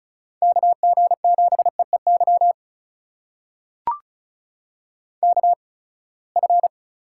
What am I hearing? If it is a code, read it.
Morse code, heard as a keyed tone, about 36 wpm, spaced as for 24 wpm, 700 Hz. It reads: KG7EEY KF
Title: USA Call Signs - 8x Character Spacing - 35wpm